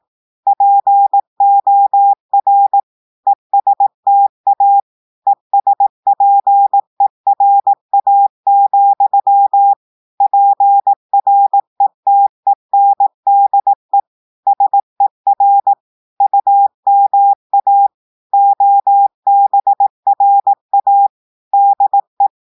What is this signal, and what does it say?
Morse code, heard as a keyed tone, about 18 wpm, 800 Hz. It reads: POR ESTA ESPERA, PRETENDE SER UMA OBRA DE
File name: audios / portugais / portugais texte en clair 1 vitesse 18 mots minutes